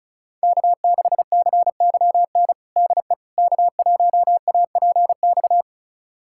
Morse code, heard as a keyed tone, about 36 wpm, 700 Hz. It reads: K6CYN DE K1APX
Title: USA Callsign DE Callsign - 5 Characters - 35wpm